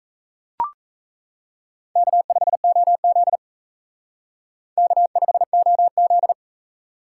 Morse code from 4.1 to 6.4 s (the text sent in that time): K5OZ